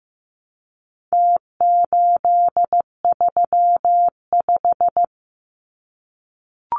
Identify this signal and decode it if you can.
Morse code, heard as a keyed tone, about 15 wpm, 700 Hz. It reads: T835
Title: Numbers - Contest Style - 4-Digit 15wpm